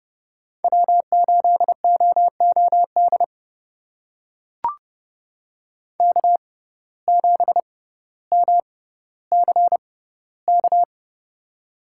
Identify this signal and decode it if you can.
Morse code, heard as a keyed tone, about 30 wpm, spaced as for 15 wpm, 700 Hz. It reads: W8OOD K7MCK
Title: USA Call Signs - 5 Characters - 6x Character Spacing - 30wpm